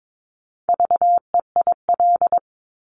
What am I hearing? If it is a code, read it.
Morse code, heard as a keyed tone, about 22 wpm, 700 Hz. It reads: VEIL
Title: Multi-letter Phonemes Repeated 3x 22wpm